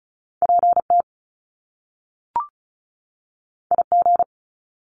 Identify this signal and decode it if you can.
Morse code, heard as a keyed tone, about 36 wpm, 700 Hz. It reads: PT IG
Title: Top Two Letter Combinations in Words 35wpm